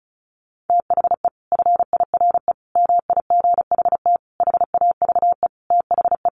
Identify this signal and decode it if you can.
Morse code, heard as a keyed tone, about 34 wpm, 700 Hz. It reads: THE FIRE MIGHT HAVE THE
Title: Sentences from Top 1300 Words 35wpm